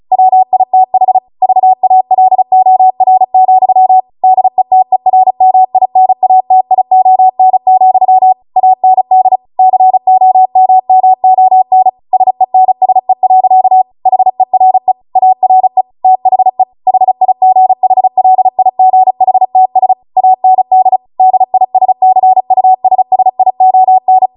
Morse code, heard as a keyed tone, about 36 wpm, 750 Hz. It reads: WITH VALOR, DETERMINATION, AND COMMON SENSE. HERE ARE THE HIGHLIGHTS AND DISCUSSION
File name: Morse / 35WPM